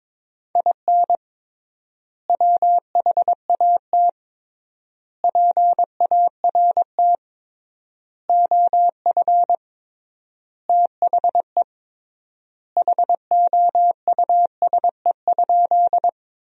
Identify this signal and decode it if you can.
Morse code, heard as a keyed tone, about 22 wpm, 700 Hz. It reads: IN WHAT PART OF THE HOUSE?
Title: 3x Word-Spacing - Sentences from Top 200 Words - 22wpm